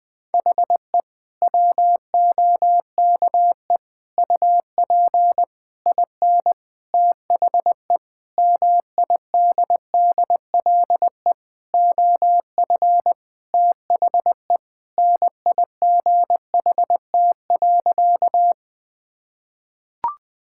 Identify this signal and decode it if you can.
Morse code, heard as a keyed tone, about 20 wpm, 700 Hz. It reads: HE WOKE UP IN THE MIDDLE OF THE NIGHT.
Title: Sentences with 6-Word Phrases - 20wpm